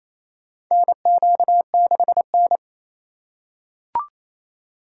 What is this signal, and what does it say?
Morse code, heard as a keyed tone, about 28 wpm, 700 Hz. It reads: NQ6N
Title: USA Call Signs - 4 Characters - 4x Character Spacing - 28wpm